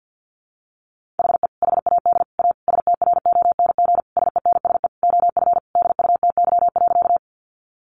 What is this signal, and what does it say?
Morse code, heard as a keyed tone, about 50 wpm, 700 Hz. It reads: HE HAD A STRONG SENSE OF DUTY.
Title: Sentences from Top 1500 Words 50wpm